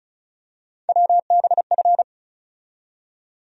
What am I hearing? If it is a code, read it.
Morse code, heard as a keyed tone, about 36 wpm, 700 Hz. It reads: WBF
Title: USA Call Sign Postfixes - 3 Letters 35wpm